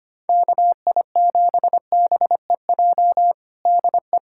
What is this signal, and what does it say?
Morse code, heard as a keyed tone, about 25 wpm, 700 Hz. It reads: KI7BEJ DE
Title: USA Callsign DE Callsign - 6 Characters - 25wpm